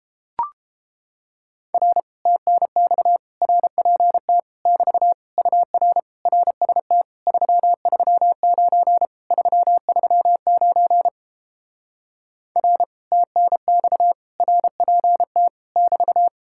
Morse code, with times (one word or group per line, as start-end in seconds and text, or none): 1.7-2.0 R
2.3-3.2 TNX
3.4-4.4 RPT
4.7-5.1 <BT>
5.4-6.0 UR
6.3-7.0 RST
7.3-9.1 339
9.3-11.1 339
12.6-12.8 R
13.1-14.1 TNX
14.4-15.5 RPT
15.8-16.3 <BT>